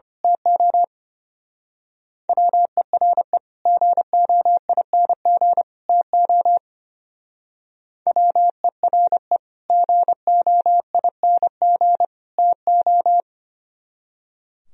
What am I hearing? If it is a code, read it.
Morse code, heard as a keyed tone, about 27 wpm, 700 Hz. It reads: TO WERE GOING TO WERE GOING TO
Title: Speed-Racing - Sets of 3 Words 25wpm